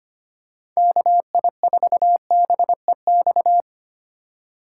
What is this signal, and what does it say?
Morse code, heard as a keyed tone, about 25 wpm, 700 Hz. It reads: KI4BEX
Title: USA Call Signs - 6 Characters - 8x Character Spacing - 25wpm